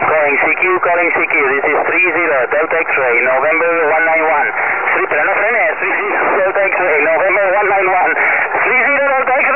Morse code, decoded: none